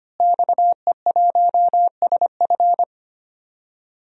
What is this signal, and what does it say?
Morse code, heard as a keyed tone, about 25 wpm, 700 Hz. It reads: XE1SF
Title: DXCC-100 Call Signs - 5 Characters - Repeated 3x - 25wpm